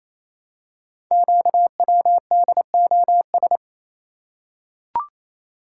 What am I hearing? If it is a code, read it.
Morse code, heard as a keyed tone, about 27 wpm, 700 Hz. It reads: QWDOS
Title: Code Groups - 5 Letters - 28wpm